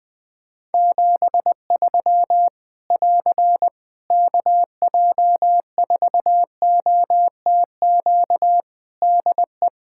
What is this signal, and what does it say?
Morse code, heard as a keyed tone, about 20 wpm, 700 Hz. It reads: 73 <AR> KJ4OTQ DE
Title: Practice QSO - Exchange 4 - Ending - 20wpm